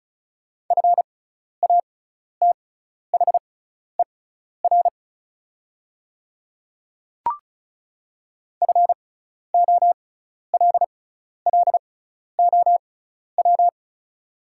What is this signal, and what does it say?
Morse code, heard as a keyed tone, about 36 wpm, spaced as for 12 wpm, 700 Hz. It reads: FATHER FOLLOW
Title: 6x Character Spacing - Top 300 Words - 35wpm